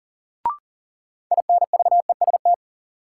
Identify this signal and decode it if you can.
Morse code, heard as a keyed tone, about 40 wpm, 700 Hz. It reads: INVEST
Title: Sentences from Top 1100 Words - New Word Drill Alt - 40wpm